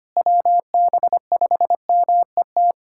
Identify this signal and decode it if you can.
Morse code, heard as a keyed tone, about 25 wpm, 700 Hz. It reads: WB5MET
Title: USA Callsign DE Callsign - 6 Characters - 25wpm